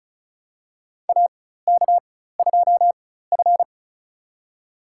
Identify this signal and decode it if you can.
Morse code, heard as a keyed tone, about 36 wpm, spaced as for 16 wpm, 700 Hz. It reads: AK2F
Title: USA Call Signs - 4x Character Spacing - 35wpm